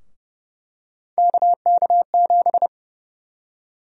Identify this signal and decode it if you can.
Morse code, heard as a keyed tone, about 30 wpm, 700 Hz. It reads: KK7
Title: USA Call Sign Prefixes - Repeated 2x - 30wpm